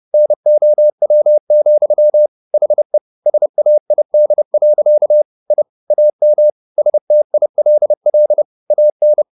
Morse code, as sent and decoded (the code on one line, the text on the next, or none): -. --- .-- --..-- / .... / ... .- .. -.. .-.-.- / .. / .- -- / ... - .. .-.. .-.. / .- -.
NOW, HE SAID. I AM STILL AN